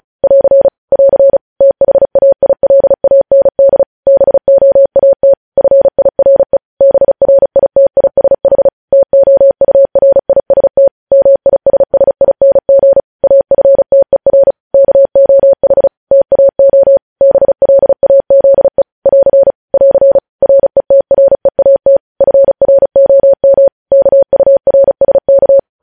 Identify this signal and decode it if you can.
Morse code, heard as a keyed tone, about 36 wpm, 550 Hz. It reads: <AR> <AR> THAILAND BOAT FIRE BRITISH TOURIST MISSING AFTER KOH TAO BLAZE <AR> <AR> RETREAT FROM KURSK
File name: podcast / News headlines-1x-35x35